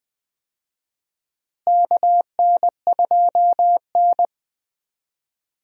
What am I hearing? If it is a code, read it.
Morse code, heard as a keyed tone, about 20 wpm, 700 Hz. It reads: KN2N
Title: USA Call Signs - 4 Characters - Repeated 3x - 20wpm